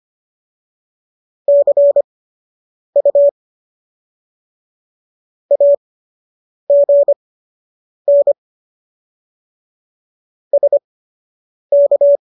Morse code, heard as a keyed tone, about 25 wpm, spaced as for 8 wpm, 570 Hz. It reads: CU AGN SK
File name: qso006 08